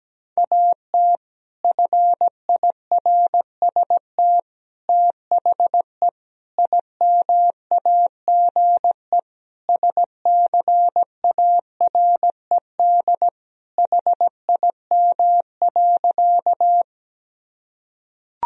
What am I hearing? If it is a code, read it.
Morse code, heard as a keyed tone, about 17 wpm, 700 Hz. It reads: AT FIRST THE IMAGE SCARED HIM.
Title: Sentences from Top 1000 Words - Review Part 4 - 17wpm